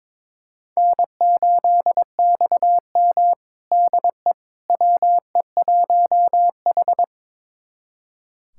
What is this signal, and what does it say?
Morse code, heard as a keyed tone, about 21 wpm, 700 Hz. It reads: N8XM DE WE1H
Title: USA Callsign DE Callsign - 4 Characters - 22wpm